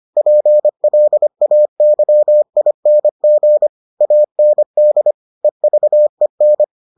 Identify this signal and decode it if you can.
Morse code, heard as a keyed tone, about 25 wpm, 600 Hz. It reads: PLAYING AND EVEN